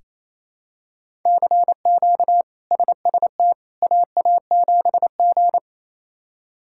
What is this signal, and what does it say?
Morse code, heard as a keyed tone, about 27 wpm, 700 Hz. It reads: CQ SST AA7G
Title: K1USN SST QSO Practice - 28wpm